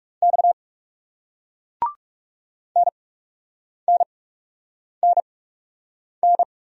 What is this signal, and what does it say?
Morse code, heard as a keyed tone, about 41 wpm, spaced as for 8 wpm, 700 Hz. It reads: X NNNN